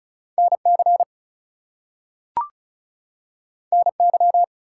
Morse code, heard as a keyed tone, about 36 wpm, 700 Hz. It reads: NC NY
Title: US States and Canadian Provinces 35wpm